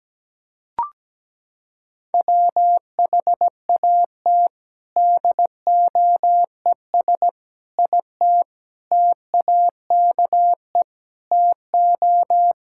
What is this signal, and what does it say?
Morse code, heard as a keyed tone, about 17 wpm, 700 Hz. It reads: WHAT DOES IT TAKE TO
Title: Sentences from Top 1600 Words 17wpm